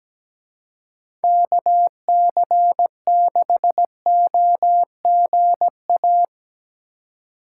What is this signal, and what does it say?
Morse code, heard as a keyed tone, about 20 wpm, 700 Hz. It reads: KC6OGA
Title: Speed Racing - USA 2x3 Call Signs 17wpm